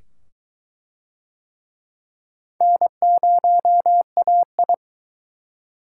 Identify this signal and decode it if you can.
Morse code, heard as a keyed tone, about 22 wpm, 700 Hz. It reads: N0AI